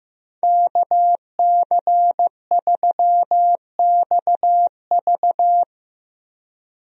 Morse code, heard as a keyed tone, about 15 wpm, 700 Hz. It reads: KC3XV
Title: USA Call Signs - 5 Characters - Repeated 2x - 15wpm